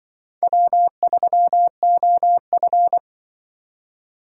Morse code, heard as a keyed tone, about 24 wpm, 700 Hz. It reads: W3OF
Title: Speed Racing - USA 1x2 Call Signs 20wpm